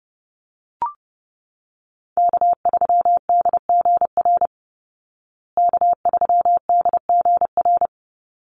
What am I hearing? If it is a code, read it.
Morse code, heard as a keyed tone, about 30 wpm, 700 Hz. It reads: K3DGR K3DGR
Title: Call Signs Repeated 2x - 30wpm